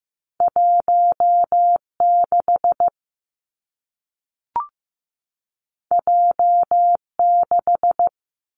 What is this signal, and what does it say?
Morse code, heard as a keyed tone, about 15 wpm, 700 Hz. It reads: 1 6 J 6
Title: Two Challenging Letter-Number Combinations 15wpm